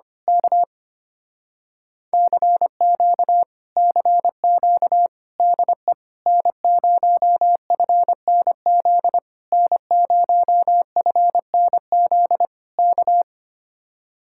Morse code, heard as a keyed tone, about 24 wpm, 700 Hz. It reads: K CQ CQ DE N0FNZ N0FNZ K
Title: Speed-Racing Common QSO Phrases Encore - 25wpm